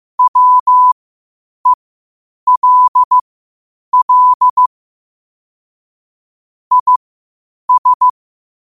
Morse code, heard as a keyed tone, about 15 wpm, spaced as for 8 wpm, 1000 Hz. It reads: WELL IS
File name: qotdfiles / 8wpm